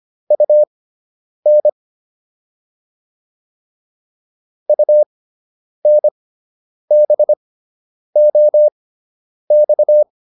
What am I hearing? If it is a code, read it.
Morse code, heard as a keyed tone, about 25 wpm, spaced as for 9 wpm, 600 Hz. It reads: UN UNBOX